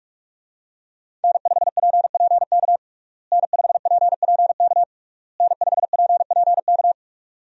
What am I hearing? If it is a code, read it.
Morse code, heard as a keyed tone, about 46 wpm, 700 Hz. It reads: N5PPK N5PPK N5PPK
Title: USA Call Signs - 5 Characters - Repeated 3x - 45wpm